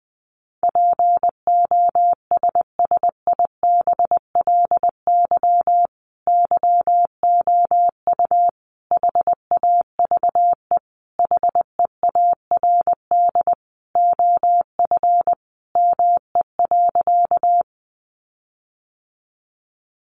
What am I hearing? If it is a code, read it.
Morse code, heard as a keyed tone, about 20 wpm, 700 Hz. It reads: POSSIBLY YOU HAVE HEARD OF ME.
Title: Sentences from Top 1300 Words 20wpm